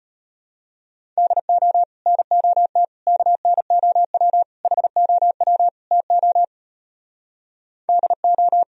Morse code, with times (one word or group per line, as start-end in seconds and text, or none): 1.2-1.8 DO
2.1-2.8 NOT
3.1-4.4 KNOW
4.6-5.7 HOW
5.9-6.4 TO
7.9-8.6 DO